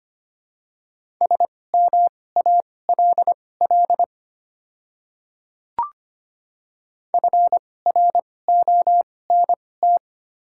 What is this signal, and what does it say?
Morse code, heard as a keyed tone, about 25 wpm, spaced as for 18 wpm, 700 Hz. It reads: SMALL FRONT
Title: 2x Character Spacing - Top 500 Words - 25wpm